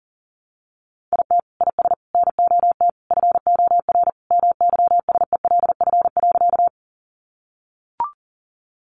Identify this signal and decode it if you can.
Morse code, heard as a keyed tone, about 40 wpm, 700 Hz. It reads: IT IS NOT FOR MYSELF.
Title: Sentences from Top 1300 Words 40wpm